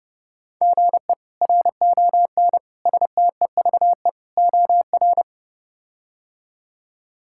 Morse code, heard as a keyed tone, about 32 wpm, 700 Hz. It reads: GE RON STEVE OR
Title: K1USN SST QSO Practice - 30wpm